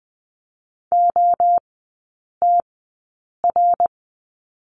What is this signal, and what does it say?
Morse code, heard as a keyed tone, about 20 wpm, spaced as for 8 wpm, 700 Hz. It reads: OTR
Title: 053 Lesson 10e - Letter r - Three Characters 20wpm